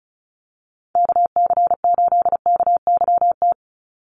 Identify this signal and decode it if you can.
Morse code, heard as a keyed tone, about 34 wpm, 700 Hz. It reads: KC8KYT
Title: USA Call Signs 35wpm